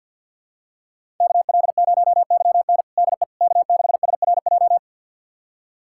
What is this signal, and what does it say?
Morse code, heard as a keyed tone, about 52 wpm, 700 Hz. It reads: KL0YN DE K6SRJ
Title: USA Callsign DE Callsign - 5 Characters - 50wpm